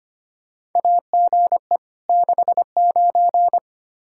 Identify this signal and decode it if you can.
Morse code, heard as a keyed tone, about 25 wpm, 700 Hz. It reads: AGE 69
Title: Practice QSO - Exchange 3 - More Info - 25wpm